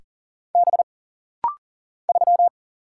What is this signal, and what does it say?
Morse code, heard as a keyed tone, about 41 wpm, 700 Hz. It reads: B 3
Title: Single Letter-Number - Rapid-Fire 40wpm